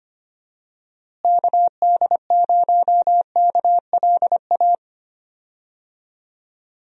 Morse code, heard as a keyed tone, about 25 wpm, 700 Hz. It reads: KD0KLA
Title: USA Call Signs - 6 Characters - Repeated 3x - 25wpm